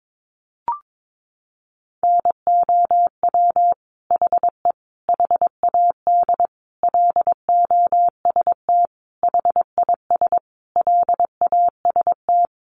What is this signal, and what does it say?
Morse code, heard as a keyed tone, about 22 wpm, 700 Hz. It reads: NOW HE HAD LOST HIS LAST
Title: Sentences from Top 1400 Words 22wpm